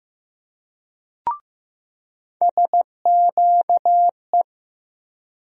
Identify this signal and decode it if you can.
Morse code, heard as a keyed tone, about 15 wpm, 700 Hz. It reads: SQE
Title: USA Call Sign Postfixes - 3 Letters 15wpm